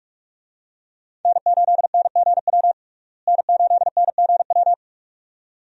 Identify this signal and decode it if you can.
Morse code, heard as a keyed tone, about 42 wpm, 700 Hz. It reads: N8NGW N8NGW
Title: USA Call Signs - 5 Characters - Repeated 2x - 45wpm